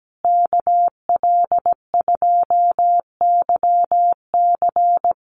Morse code, heard as a keyed tone, about 17 wpm, 700 Hz. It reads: KL2YC
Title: USA Call Signs Repeated 3x 17wpm